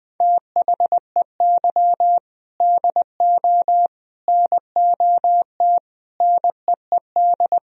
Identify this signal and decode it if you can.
Morse code, heard as a keyed tone, about 21 wpm, 700 Hz. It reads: THEY DO NOT NEED
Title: Sentences from Top 1000 Words - Review Part 1 - 20wpm